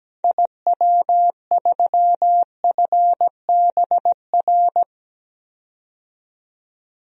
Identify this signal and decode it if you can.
Morse code, heard as a keyed tone, about 17 wpm, 700 Hz. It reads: IW3FBR
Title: DXCC-100 Call Signs - 6 Characters - 17wpm